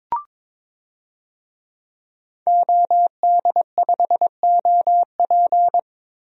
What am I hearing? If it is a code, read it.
Morse code, heard as a keyed tone, about 22 wpm, 700 Hz. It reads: OD5OP
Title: DXCC-100 Call Signs - 5 Characters - Repeated 2x - 22wpm